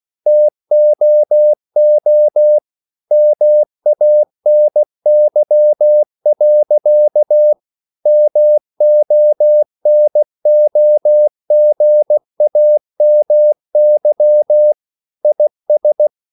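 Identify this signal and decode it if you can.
Morse code, heard as a keyed tone, about 16 wpm, 600 Hz. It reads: TOO MANY. MONOGAMY IS
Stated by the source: English quotations sent in CW as practice at 16 words per minute.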